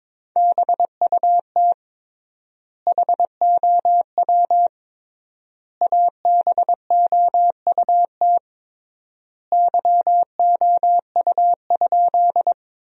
Morse code, heard as a keyed tone, about 22 wpm, 700 Hz. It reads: BUT HOW ABOUT YOU?
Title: 3x Word-Spacing - Sentences from Top 100 Words - 22wpm